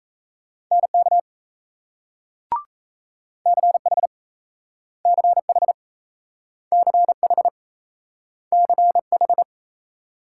Morse code, heard as a keyed tone, about 38 wpm, 700 Hz. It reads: NK CH CH CH CH